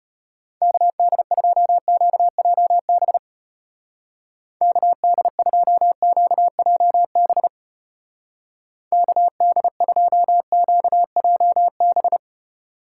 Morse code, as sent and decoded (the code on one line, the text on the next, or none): -.- -.. ..--- --.- .--- -... / -.- -.. ..--- --.- .--- -... / -.- -.. ..--- --.- .--- -...
KD2QJB KD2QJB KD2QJB